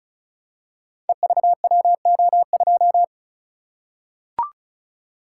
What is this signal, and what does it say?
Morse code, heard as a keyed tone, about 36 wpm, 700 Hz. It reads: EVWO2
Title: Code Groups - 5 Letter:Numbers - 35wpm